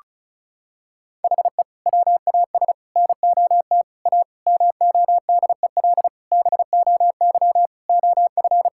Morse code, heard as a keyed tone, about 36 wpm, 700 Hz. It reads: HE WAS NOT A MODEL BOY OF
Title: Sentences from Top 1000 Words - Review Part 5 - 35wpm